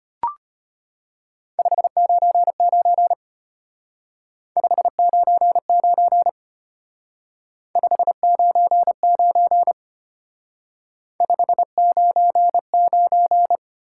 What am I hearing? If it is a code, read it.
Morse code, heard as a keyed tone, about 32 wpm, 700 Hz. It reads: 599 599 599 599